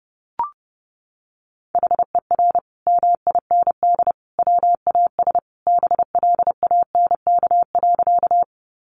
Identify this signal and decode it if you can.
Morse code, heard as a keyed tone, about 30 wpm, 700 Hz. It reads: HER MIND WAS BLANK.
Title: Sentences from Top 1500 Words 30wpm